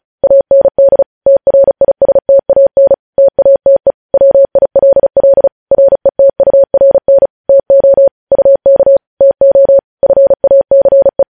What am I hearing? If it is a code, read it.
Morse code, heard as a keyed tone, about 36 wpm, 550 Hz. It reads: AND TRISTAN TATE WILL RETURN TO UK TO FACE
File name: podcast / News headlines-1x-35x35